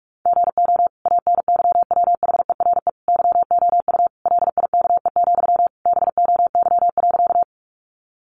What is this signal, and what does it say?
Morse code, heard as a keyed tone, about 44 wpm, 700 Hz. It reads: GO ANYWHERE YOU LIKE, BOY.
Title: Sentences from Top 1400 Words 45wpm